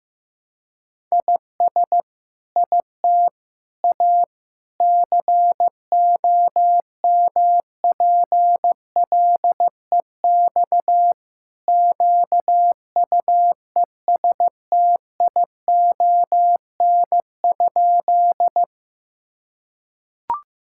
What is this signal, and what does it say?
Morse code, heard as a keyed tone, about 15 wpm, 700 Hz. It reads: IS IT A COMPLEX QUESTION?
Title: Sentences from Top 1000 Words - Review Part 6 - 15wpm